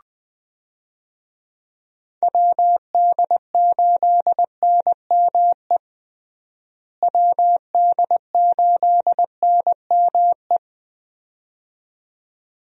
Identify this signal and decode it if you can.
Morse code, heard as a keyed tone, about 21 wpm, 700 Hz. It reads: WD8NME WD8NME
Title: USA Call Signs - 6 Characters - Repeated 2x - 20wpm